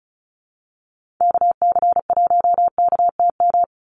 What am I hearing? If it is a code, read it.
Morse code, heard as a keyed tone, about 34 wpm, 700 Hz. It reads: KC1KTM
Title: USA Call Signs 35wpm